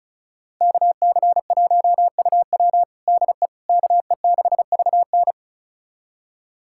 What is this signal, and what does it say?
Morse code, heard as a keyed tone, about 36 wpm, 700 Hz. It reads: KC1UW DE KE6VN
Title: USA Callsign DE Callsign - 5 Characters - 35wpm